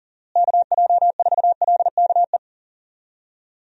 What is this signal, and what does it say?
Morse code, heard as a keyed tone, about 40 wpm, 700 Hz. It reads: KJ4LKE
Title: USA Call Signs - 6 Characters - Repeated 2x - 40wpm